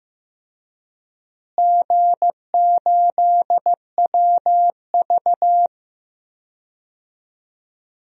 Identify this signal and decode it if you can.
Morse code, heard as a keyed tone, about 15 wpm, 700 Hz. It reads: G8WV
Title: DXCC-100 Call Signs - 4 Characters - Repeated 3x - 15wpm